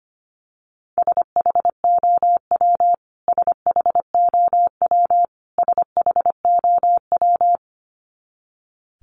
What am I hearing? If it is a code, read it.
Morse code, heard as a keyed tone, about 25 wpm, 700 Hz. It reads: SHOW SHOW SHOW
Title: Top 500 Words Repeated 25wpm